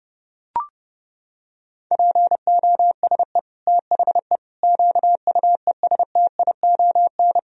Morse code, heard as a keyed tone, about 30 wpm, 700 Hz. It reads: POSE THE QUESTION
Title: Sentences from Top 1000 Words - Review Part 2 - 30wpm